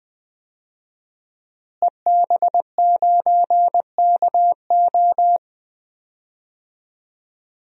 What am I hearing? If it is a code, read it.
Morse code, heard as a keyed tone, about 20 wpm, 700 Hz. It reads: EB9KO
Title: DXCC-100 Call Signs - 5 Characters - Repeated 3x - 20wpm